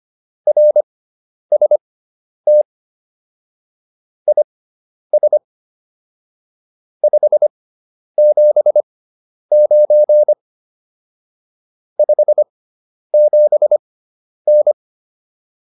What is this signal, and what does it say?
Morse code, heard as a keyed tone, about 25 wpm, spaced as for 10 wpm, 600 Hz. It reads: RST IS 579 57N